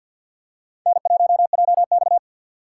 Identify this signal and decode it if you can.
Morse code, heard as a keyed tone, about 50 wpm, 700 Hz. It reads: N1JX
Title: USA Call Signs - 4 Characters - 3x Character Spacing - 50wpm